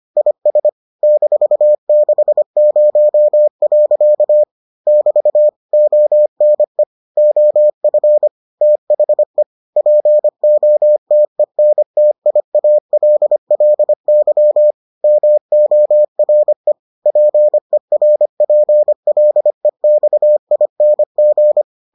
Morse code, heard as a keyed tone, about 25 wpm, 600 Hz. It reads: IS -60. <BT> ONE OF THE POTENTIALLY MORE PERPLEXING